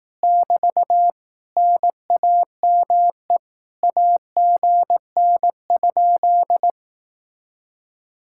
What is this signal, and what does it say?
Morse code, heard as a keyed tone, about 18 wpm, 700 Hz. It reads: <BT> NAME AGN?
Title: Speed-Racing Common QSO Phrases Encore - 15wpm